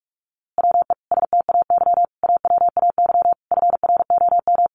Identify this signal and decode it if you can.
Morse code, heard as a keyed tone, about 44 wpm, 700 Hz. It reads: WE STAY AWAY FROM
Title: A Sound Of Thunder 45wpm